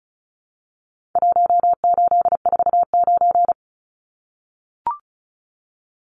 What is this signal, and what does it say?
Morse code, heard as a keyed tone, about 34 wpm, 700 Hz. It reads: 1849
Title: Numbers - Contest Style - 4-Digit 35wpm